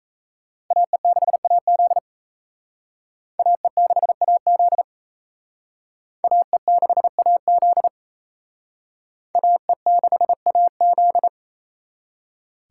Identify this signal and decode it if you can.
Morse code, heard as a keyed tone, about 35 wpm, 700 Hz. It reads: AE6AZ AE6AZ AE6AZ AE6AZ
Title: Speed Racing - USA 2x2 Call Signs 28wpm